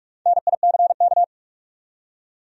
Morse code, heard as a keyed tone, about 44 wpm, 700 Hz. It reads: NICK
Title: USA Common Names - Max 4 Letters - 45wpm